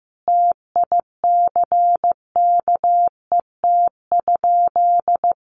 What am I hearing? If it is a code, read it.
Morse code, heard as a keyed tone, about 15 wpm, 700 Hz. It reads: TICKET?
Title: Sentences from Top 1400 Words 15wpm